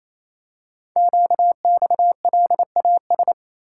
Morse code, heard as a keyed tone, about 27 wpm, 700 Hz. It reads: QXLAS
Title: Code Groups - 5 Letters - 28wpm